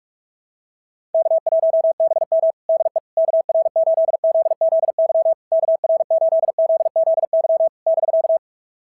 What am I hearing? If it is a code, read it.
Morse code, heard as a keyed tone, about 44 wpm, 690 Hz. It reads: K1BM DE KR8ZZY KR8ZZY <BK>
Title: SOTA QSO Practice 45wpm